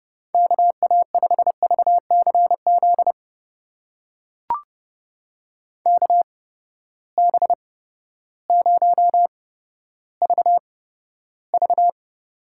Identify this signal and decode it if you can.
Morse code, heard as a keyed tone, about 30 wpm, spaced as for 14 wpm, 700 Hz. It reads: KA5VCZ KB0VV
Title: USA Call Signs - 8x Character Spacing - 30wpm